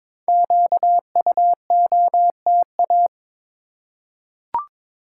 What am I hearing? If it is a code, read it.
Morse code, heard as a keyed tone, about 22 wpm, 700 Hz. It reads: QUOTA
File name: 5 Letter Words - First 3 Letters Given - 22wpm